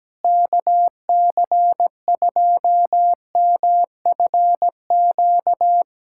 Morse code, heard as a keyed tone, about 17 wpm, 700 Hz. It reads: KC2MFQ
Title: USA Callsign DE Callsign 17wpm